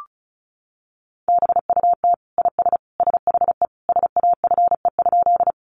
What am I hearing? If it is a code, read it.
Morse code, heard as a keyed tone, about 34 wpm, 700 Hz. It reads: BUT IS SHE SAFE?
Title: Sentences from Top 900 Words 35wpm